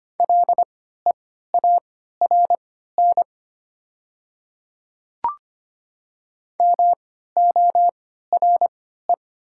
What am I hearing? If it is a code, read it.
Morse code, heard as a keyed tone, about 25 wpm, spaced as for 14 wpm, 700 Hz. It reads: LEARN MORE
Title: 3x Character Spacing - Top 500 Words - 25wpm